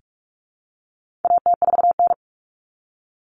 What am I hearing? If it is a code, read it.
Morse code, heard as a keyed tone, about 44 wpm, 700 Hz. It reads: AT4N
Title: Common Cut Numbers - 4-Digit 45wpm